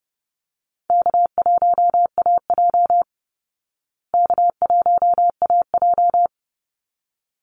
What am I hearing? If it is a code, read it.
Morse code, heard as a keyed tone, about 30 wpm, 700 Hz. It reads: K1AJ K1AJ
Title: Easy Call Signs Repeated 2x - 30wpm